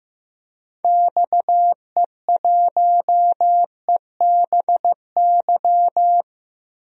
Morse code, heard as a keyed tone, about 15 wpm, 700 Hz. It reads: XE1EBY
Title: DXCC-100 Call Signs - 6 Characters - 15wpm